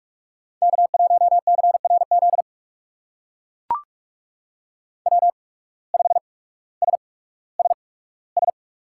Morse code, heard as a keyed tone, about 44 wpm, spaced as for 19 wpm, 700 Hz. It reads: K1CRZ W5SSS
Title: USA Call Signs - 5 Characters - 8x Character Spacing - 45wpm